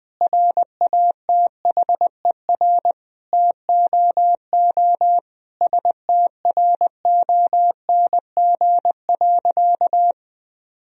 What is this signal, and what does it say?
Morse code, heard as a keyed tone, about 20 wpm, 700 Hz. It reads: RATHER TOO STRONG.
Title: Sentences from Top 1000 Words - Review Part 4 - 20wpm